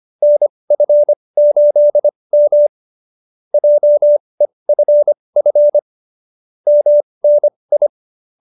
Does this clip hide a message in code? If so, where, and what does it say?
Yes: Morse code from 0.2 to 7.9 s, heard as a keyed tone, about 25 wpm, spaced as for 20 wpm, 570 Hz. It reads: NF8M JEFF MNI